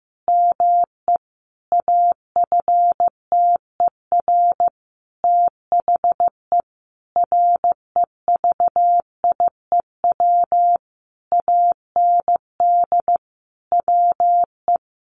Morse code, heard as a keyed tone, about 15 wpm, 700 Hz. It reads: ME AFTER THE REVIEW AND WE
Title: Sentences from Top 900 Words 15wpm